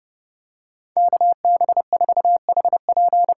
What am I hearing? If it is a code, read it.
Morse code, heard as a keyed tone, about 32 wpm, 700 Hz. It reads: KB4HP
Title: USA Call Signs - 8x Character Spacing - 30wpm